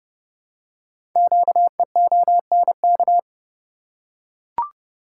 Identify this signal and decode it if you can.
Morse code, heard as a keyed tone, about 30 wpm, 700 Hz. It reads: QEONK
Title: Code Groups - 5 Letters - 30wpm